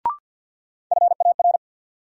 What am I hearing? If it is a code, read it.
Morse code, heard as a keyed tone, about 47 wpm, 700 Hz. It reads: FAR
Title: Sentences from Top 300 Words - New Word Drill - 50wpm